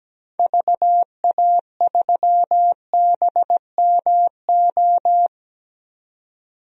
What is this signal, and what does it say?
Morse code, heard as a keyed tone, about 17 wpm, 700 Hz. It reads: VA3BMO
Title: DXCC-100 Call Signs - 6 Characters - 17wpm